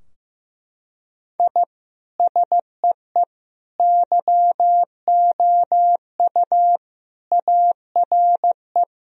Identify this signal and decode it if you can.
Morse code, heard as a keyed tone, about 15 wpm, 700 Hz. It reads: I SEE YOU ARE
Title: Sentences from Top 1000 Words - Review Part 3 - 15wpm